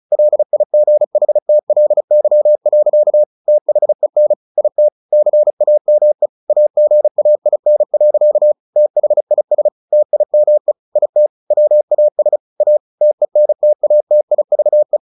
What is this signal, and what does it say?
Morse code, heard as a keyed tone, about 34 wpm, 600 Hz. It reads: LIGHTLY. THEN IT CAME AGAIN. THIS TIME IT WAS A TENTATIVE
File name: the old man and the sea / the old man and the sea 1 35wpm